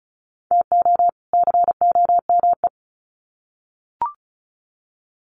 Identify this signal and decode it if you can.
Morse code, heard as a keyed tone, about 34 wpm, 700 Hz. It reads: TO COME
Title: Sets of 2 Words 35wpm